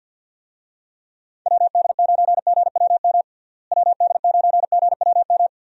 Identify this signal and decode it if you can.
Morse code, heard as a keyed tone, about 51 wpm, 700 Hz. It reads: WD9GWM WD9GWM
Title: USA Call Signs - 6 Characters - Repeated 2x - 50wpm